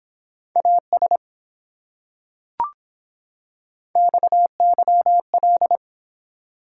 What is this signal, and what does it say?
Morse code, heard as a keyed tone, about 24 wpm, 700 Hz. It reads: AS XYL